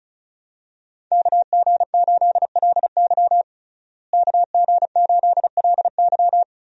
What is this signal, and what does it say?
Morse code, heard as a keyed tone, about 34 wpm, 700 Hz. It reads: KG8LY KG8LY
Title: USA Call Signs - 5 Characters - Repeated 3x - 35wpm